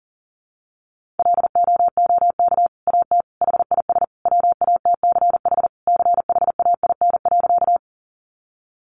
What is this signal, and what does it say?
Morse code, heard as a keyed tone, about 40 wpm, 700 Hz. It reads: LOOK AT HIS WATCH CHAIN.
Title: Sentences from Top 1500 Words 40wpm